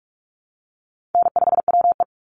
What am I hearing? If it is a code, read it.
Morse code, heard as a keyed tone, about 44 wpm, 700 Hz. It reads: N5WE
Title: Easy Call Signs Repeated 3x - 45wpm